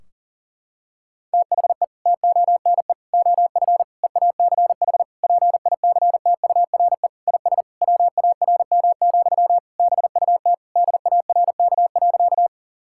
Morse code, heard as a keyed tone, about 40 wpm, 700 Hz. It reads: THE TONE OF EACH PICTURE IS WARM, BUT DARK.
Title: Sentences from Top 1600 Words 40wpm